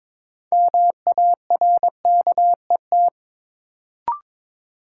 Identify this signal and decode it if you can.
Morse code, heard as a keyed tone, about 22 wpm, 700 Hz. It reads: MARKET
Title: Sentences from Top 600 Words - New Word Drill - 22wpm